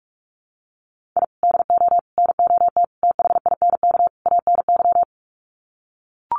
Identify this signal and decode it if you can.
Morse code, heard as a keyed tone, about 44 wpm, 700 Hz. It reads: I DO NOT THINK ANY
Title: Sets of 5 Words Encore 45wpm